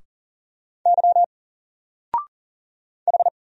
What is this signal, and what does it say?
Morse code, heard as a keyed tone, about 40 wpm, 700 Hz. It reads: Y H
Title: Single Letter-Number 40wpm